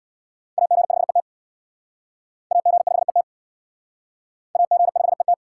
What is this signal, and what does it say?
Morse code, heard as a keyed tone, about 68 wpm, 700 Hz. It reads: AG5ET AG5ET AG5ET